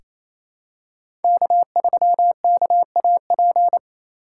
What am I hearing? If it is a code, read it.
Morse code, heard as a keyed tone, about 27 wpm, 700 Hz. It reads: K3KAP
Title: USA Call Signs - 5 Characters - 4x Character Spacing - 28wpm